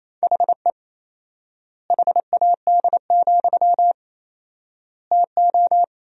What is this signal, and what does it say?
Morse code, heard as a keyed tone, about 28 wpm, 700 Hz. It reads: HE HAD, TO
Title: 4x Word-Spacing - Sentences from Top 500 Words - 28wpm